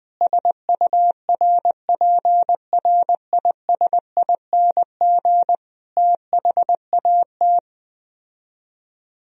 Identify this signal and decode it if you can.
Morse code, heard as a keyed tone, about 20 wpm, 700 Hz. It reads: SURPRISING THAT
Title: Speed-Racing - Sets of 5 Words 17wpm